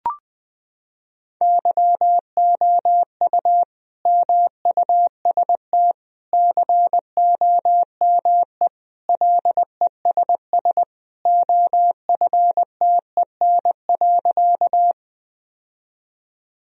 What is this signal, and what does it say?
Morse code, heard as a keyed tone, about 20 wpm, 700 Hz. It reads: YOU MUST COME LESS OFTEN.
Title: Sentences from Top 1000 Words - Review Part 2 - 20wpm